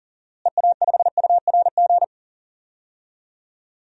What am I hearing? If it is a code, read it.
Morse code, heard as a keyed tone, about 40 wpm, 700 Hz. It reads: EA5URG
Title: DXCC-100 Call Signs - 6 Characters - Repeated 3x - 40wpm